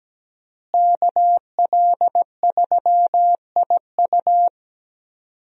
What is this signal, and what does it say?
Morse code, heard as a keyed tone, about 17 wpm, 700 Hz. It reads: KL3IU
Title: USA Call Signs - 5 Characters - Repeated 3x - 17wpm